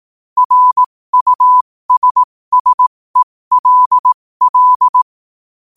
Morse code, heard as a keyed tone, about 18 wpm, spaced as for 15 wpm, 1000 Hz. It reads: RUSSELL
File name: qotdfiles / 15wpm